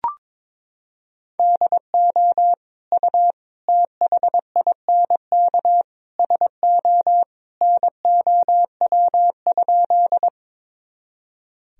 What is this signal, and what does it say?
Morse code, heard as a keyed tone, about 22 wpm, 700 Hz. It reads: DO U THINK SO NOW?
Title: Sentences from Top 200 Words and Abbreviations 22wpm